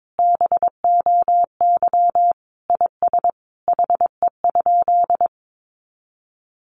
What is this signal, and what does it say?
Morse code, heard as a keyed tone, about 22 wpm, 700 Hz. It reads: BOY IS HE?
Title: Sentences from Top 1300 Words 22wpm